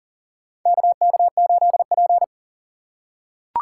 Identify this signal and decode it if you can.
Morse code, heard as a keyed tone, about 40 wpm, 700 Hz. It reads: KK8P
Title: USA Call Signs - 4 Characters - 2x Character Spacing - 40wpm